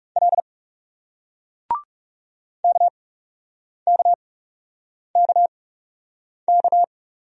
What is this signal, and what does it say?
Morse code, heard as a keyed tone, about 40 wpm, spaced as for 8 wpm, 700 Hz. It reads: L KKKK